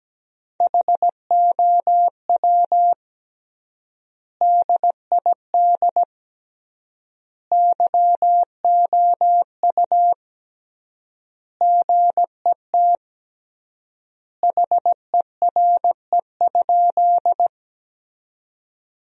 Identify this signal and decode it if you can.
Morse code, heard as a keyed tone, about 17 wpm, 700 Hz. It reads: HOW DID YOU GET HERE?
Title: 3x Word-Spacing - Sentences from Top 200 Words - 17wpm